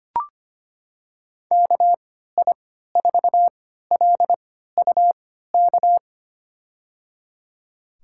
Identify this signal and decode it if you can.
Morse code, heard as a keyed tone, about 25 wpm, spaced as for 14 wpm, 700 Hz. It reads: KI4LUK
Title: USA Call Signs - 6 Characters - 3x Character Spacing - 25wpm